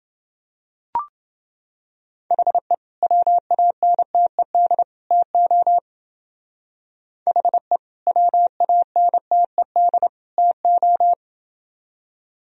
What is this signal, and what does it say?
Morse code, heard as a keyed tone, about 29 wpm, 700 Hz. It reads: HE WANTED TO HE WANTED TO